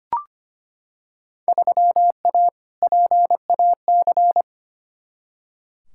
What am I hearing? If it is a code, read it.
Morse code, heard as a keyed tone, about 25 wpm, 700 Hz. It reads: 3A PAC
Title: Field Day Exchanges Encore - 25wpm